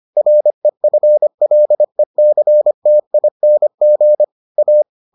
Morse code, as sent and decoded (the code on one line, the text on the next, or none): .-. ..-. .-.. -.-. - .. -. --. / .-
REFLECTING A